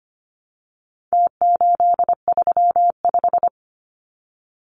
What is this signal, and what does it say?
Morse code, heard as a keyed tone, about 25 wpm, 700 Hz. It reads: T835
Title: Numbers - Contest Style - 4-Digit 25wpm